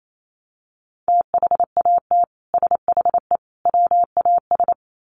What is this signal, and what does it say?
Morse code, heard as a keyed tone, about 28 wpm, 700 Hz. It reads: THAT SHE WAS
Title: Sets of 3 Words 28wpm